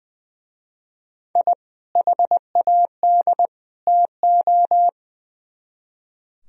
Morse code, heard as a keyed tone, about 25 wpm, 700 Hz. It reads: I HAD TO
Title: Speed-Racing - Sets of 3 Words 20wpm